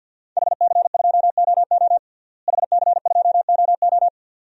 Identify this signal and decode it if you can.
Morse code, heard as a keyed tone, about 50 wpm, 700 Hz. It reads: HC2OO HC2OO
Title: DXCC-100 Call Signs - 5 Characters - Repeated 2x - 50wpm